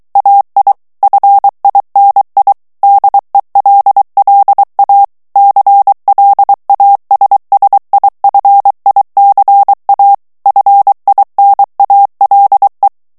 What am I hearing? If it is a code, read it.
Morse code, heard as a keyed tone, about 23 wpm, 800 Hz. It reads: AI FINI DELLA CLASSIFICA FINALE
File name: ITALIEN texte clair 2 vitesse 22 mots minute